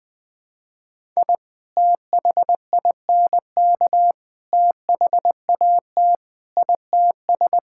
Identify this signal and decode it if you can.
Morse code, heard as a keyed tone, about 20 wpm, 700 Hz. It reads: I THINK THAT ITS
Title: Sentences with 2-Word Phrases - Spoken Beginning - 20wpm